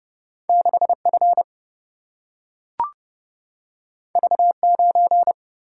Morse code, heard as a keyed tone, about 30 wpm, 700 Hz. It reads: 6F V9
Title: Single-Letter Single-Number 30wpm